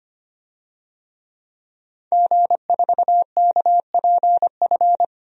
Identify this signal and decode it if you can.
Morse code, heard as a keyed tone, about 25 wpm, 700 Hz. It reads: G4KPF
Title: DXCC-100 Call Signs - 5 Characters - Repeated 3x - 25wpm